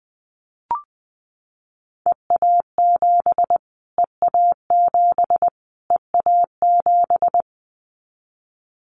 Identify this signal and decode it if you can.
Morse code, heard as a keyed tone, about 20 wpm, 700 Hz. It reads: EA7 EA7 EA7
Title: Callsign Prefixes for Top 25 DXCC - Repeated 3x - 20wpm